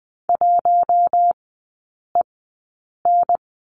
Morse code, heard as a keyed tone, about 20 wpm, spaced as for 8 wpm, 700 Hz. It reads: 1EN
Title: 053 Lesson 10e - Letter r - Three Characters 20wpm